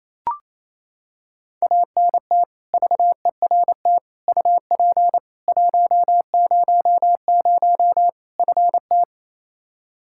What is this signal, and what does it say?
Morse code, heard as a keyed tone, about 28 wpm, 700 Hz. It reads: ANT VERT UP 100 FT
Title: Practice QSO - Exchange 2 - Info - 28wpm